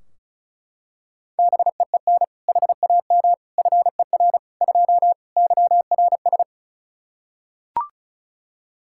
Morse code, heard as a keyed tone, about 36 wpm, 700 Hz. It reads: BEEN HAM FER 2 YRS
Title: K1USN SST QSO Practice 35wpm